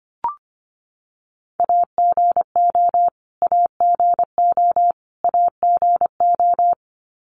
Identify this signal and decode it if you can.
Morse code, heard as a keyed tone, about 24 wpm, 700 Hz. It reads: AGO AGO AGO
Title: Top 500 Words Repeated 25wpm